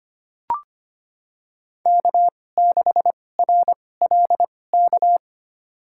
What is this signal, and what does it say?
Morse code, heard as a keyed tone, about 25 wpm, spaced as for 18 wpm, 700 Hz. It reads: K6RLK
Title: USA Call Signs - 2x Character Spacing - 25wpm